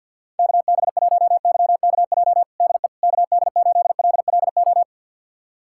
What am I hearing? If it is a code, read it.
Morse code, heard as a keyed tone, about 52 wpm, 700 Hz. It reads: KB1YKJ DE KD8LLO
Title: USA Callsign DE Callsign - 6 Characters - 50wpm